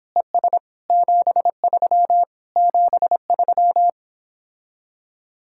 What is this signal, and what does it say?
Morse code, heard as a keyed tone, about 26 wpm, 700 Hz. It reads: ES 73 73